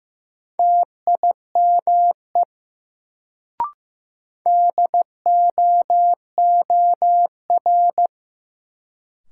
Morse code, heard as a keyed tone, about 15 wpm, 700 Hz. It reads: TIME DOOR
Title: Sets of 2 Words - New Word Drill - 15wpm